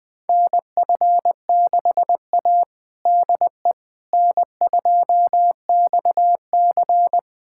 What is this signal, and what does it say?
Morse code, heard as a keyed tone, about 20 wpm, 700 Hz. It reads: NF6A DE N2XC
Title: USA Callsign DE Callsign - 4 Characters - 20wpm